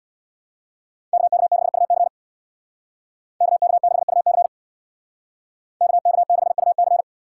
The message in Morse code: KK6UC KK6UC KK6UC

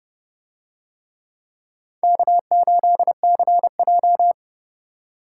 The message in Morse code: K8CJ

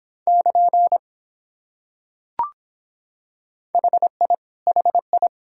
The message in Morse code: <KN> HI HI

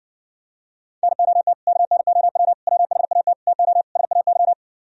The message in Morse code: NOT KNOW WHAT TO SAY

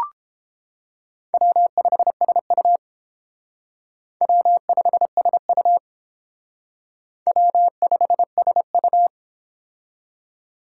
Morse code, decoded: W5SU W5SU W5SU

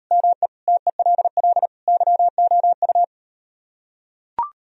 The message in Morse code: ME TELL YOU

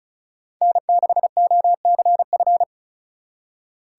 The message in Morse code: N6OCF

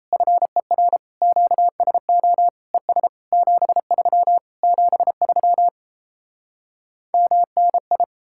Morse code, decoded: FER QSO ES 73 73 MNI

